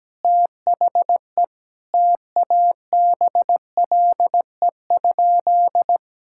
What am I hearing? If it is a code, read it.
Morse code, heard as a keyed tone, about 17 wpm, 700 Hz. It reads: THE TABLE?
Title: Sentences from Top 1000 Words - Review Part 5 - 17wpm